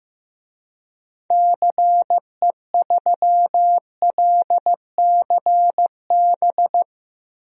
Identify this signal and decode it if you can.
Morse code, heard as a keyed tone, about 16 wpm, 700 Hz. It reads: CE3LCB
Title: DXCC-100 Call Signs - 6 Characters - 15wpm